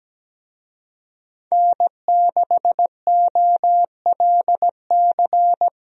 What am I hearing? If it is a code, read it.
Morse code, heard as a keyed tone, about 17 wpm, 700 Hz. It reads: N6OLC
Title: USA Call Signs - 5 Characters - Repeated 2x - 17wpm